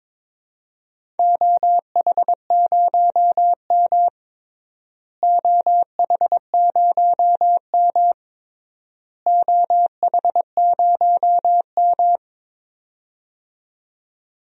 Morse code, heard as a keyed tone, about 22 wpm, 700 Hz. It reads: OH0M OH0M OH0M
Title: DXCC-100 Call Signs - 4 Characters - Repeated 3x - 22wpm